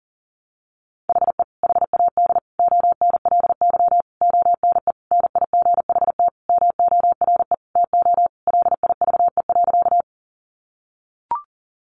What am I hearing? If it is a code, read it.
Morse code, heard as a keyed tone, about 40 wpm, 700 Hz. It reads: HE HAD ONLY ONE NIGHT MORE TO LIVE.
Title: Sentences from Top 300 Words 40wpm